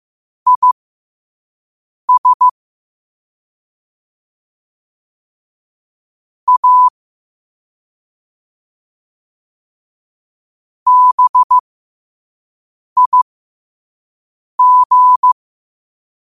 Morse code, heard as a keyed tone, about 15 wpm, spaced as for 5 wpm, 1000 Hz. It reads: IS A BIG
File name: qotdfiles / 5wpm